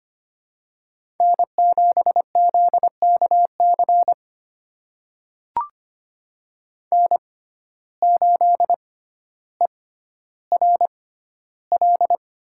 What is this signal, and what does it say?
Morse code, heard as a keyed tone, about 25 wpm, spaced as for 13 wpm, 700 Hz. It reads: N7ZKC N8ERL
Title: USA Call Signs - 5 Characters - 6x Character Spacing - 25wpm